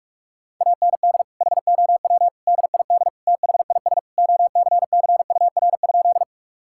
Morse code, heard as a keyed tone, about 41 wpm, 700 Hz. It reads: AND HOW DID THIS OCCUR?
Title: Sentences from Top 1000 Words - Review Part 6 - 45wpm